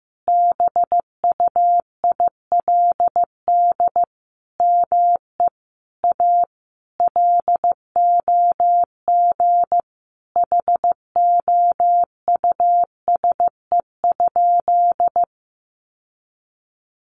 Morse code, heard as a keyed tone, about 15 wpm, 700 Hz. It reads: BUILD ME A LOG HOUSE?
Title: Sentences from Top 1400 Words 15wpm